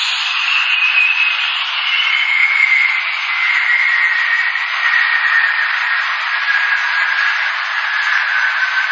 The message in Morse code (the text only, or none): none